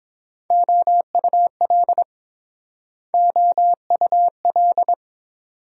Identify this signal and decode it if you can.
Morse code, heard as a keyed tone, about 30 wpm, 700 Hz. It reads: OUL OUL